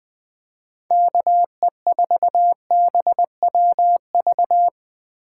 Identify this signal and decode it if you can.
Morse code, heard as a keyed tone, about 20 wpm, 700 Hz. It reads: KE4BWV
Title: USA Call Signs - 4x Character Spacing - 20wpm